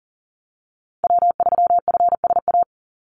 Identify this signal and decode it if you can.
Morse code, heard as a keyed tone, about 38 wpm, 700 Hz. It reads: W3FSA
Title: Call Signs Repeated 3x - 40wpm